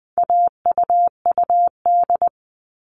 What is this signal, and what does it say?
Morse code, heard as a keyed tone, about 20 wpm, 700 Hz. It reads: AUUD
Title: All Cut Numbers - 4-Digit 20wpm